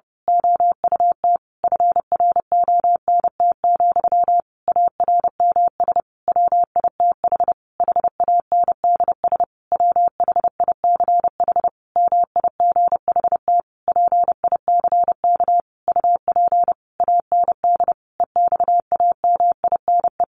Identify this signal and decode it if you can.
Morse code, heard as a keyed tone, about 30 wpm, 700 Hz. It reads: OUT FRONT, ARMS WITH HANDS WHICH MIGHT PICK UP AND EXAMINE
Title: A Sound Of Thunder 30wpm